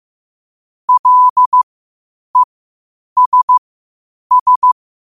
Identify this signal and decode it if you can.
Morse code, heard as a keyed tone, about 15 wpm, spaced as for 8 wpm, 1000 Hz. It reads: LESS
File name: qotdfiles / 8wpm